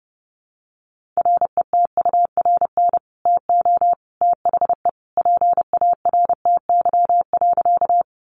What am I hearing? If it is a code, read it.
Morse code, heard as a keyed tone, about 30 wpm, 700 Hz. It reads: RETURN TO THE PARTY.
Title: Sentences from Top 1400 Words 30wpm